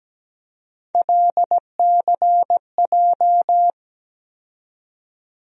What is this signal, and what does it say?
Morse code, heard as a keyed tone, about 17 wpm, 700 Hz. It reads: LCJ
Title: USA Call Sign Postfixes - 3 Letters 17wpm